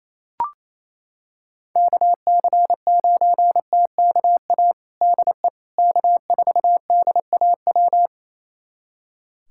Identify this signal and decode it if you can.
Morse code, heard as a keyed tone, about 28 wpm, 700 Hz. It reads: KC9TKA DE K4DAW
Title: USA Callsign DE Callsign 28wpm